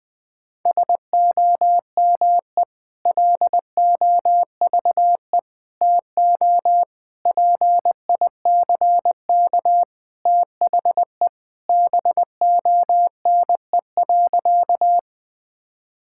Morse code, heard as a keyed tone, about 21 wpm, 700 Hz. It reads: SOME LOVE TO PICK THE BONE.
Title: Sentences from Top 1600 Words 20wpm